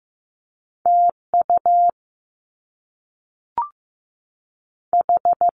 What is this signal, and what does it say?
Morse code, heard as a keyed tone, about 15 wpm, 700 Hz. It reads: TU H
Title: Top Two Letter Combinations in Words 15wpm